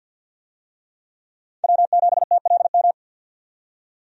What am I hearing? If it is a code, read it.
Morse code, heard as a keyed tone, about 52 wpm, 700 Hz. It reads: W7TLM
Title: USA Call Signs - 5 Characters - 50wpm